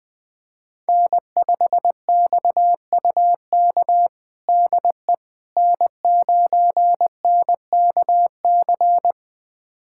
Morse code, heard as a keyed tone, about 20 wpm, 700 Hz. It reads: N5XUK DE N9NKC
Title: USA Callsign DE Callsign - 5 Characters - 20wpm